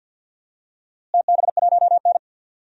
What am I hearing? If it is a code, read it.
Morse code, heard as a keyed tone, about 50 wpm, 700 Hz. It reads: TB1N
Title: DXCC-100 Call Signs - 4 Characters - Repeated 3x - 50wpm